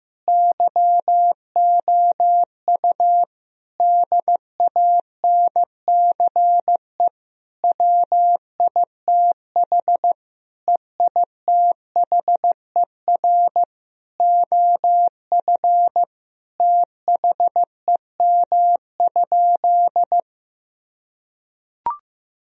Morse code, heard as a keyed tone, about 15 wpm, 700 Hz. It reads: YOU DANCE WITH EITHER OF THEM?
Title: Sentences from Top 1000 Words - Review Part 4 - 15wpm